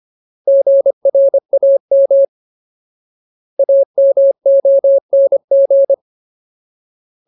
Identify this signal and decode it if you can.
Morse code, heard as a keyed tone, about 25 wpm, 550 Hz. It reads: GRAM AMONG
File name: Ultimate Word List / 25WPM Ultimate